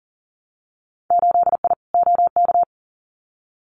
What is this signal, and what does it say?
Morse code, heard as a keyed tone, about 40 wpm, 700 Hz. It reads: 8I OK
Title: Winter Field Day Exchanges Encore 40wpm